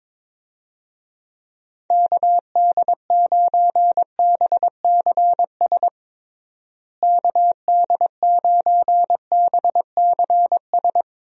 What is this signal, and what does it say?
Morse code, heard as a keyed tone, about 22 wpm, 700 Hz. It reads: KD9BCS KD9BCS
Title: USA Call Signs - 6 Characters - Repeated 3x - 22wpm